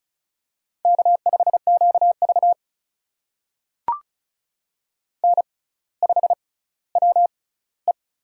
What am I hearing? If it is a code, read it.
Morse code, heard as a keyed tone, about 36 wpm, spaced as for 18 wpm, 700 Hz. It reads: K5QV N5WE
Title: USA Call Signs - 6x Character Spacing - 35wpm